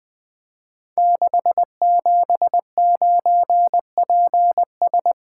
Morse code, heard as a keyed tone, about 20 wpm, 700 Hz. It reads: 679PS
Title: Code Groups - 5 Letter:Numbers - 20wpm